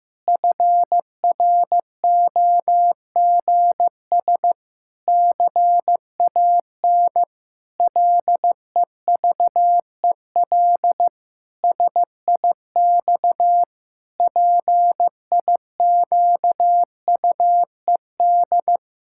Morse code, heard as a keyed tone, about 15 wpm, 700 Hz. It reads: FROGS CAN LEVEL SIX PIQUED
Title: Pangram Challenge v2 - 15wpm